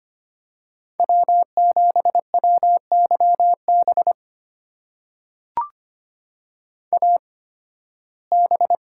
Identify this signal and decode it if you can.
Morse code, heard as a keyed tone, about 25 wpm, spaced as for 16 wpm, 700 Hz. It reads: W7WYB AB
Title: USA Call Signs - 5 Characters - 8x Character Spacing - 25wpm